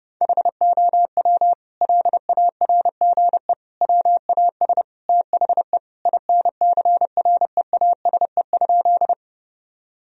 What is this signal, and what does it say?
Morse code, heard as a keyed tone, about 30 wpm, 700 Hz. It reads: HOW LARGE WAS THE INCREASE?
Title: Sentences from Top 1000 Words - Review Part 4 - 30wpm